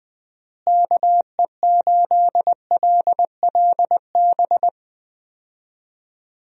Morse code, heard as a keyed tone, about 19 wpm, 700 Hz. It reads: KE8LLB
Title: USA Call Signs - 6 Characters - Repeated 2x - 20wpm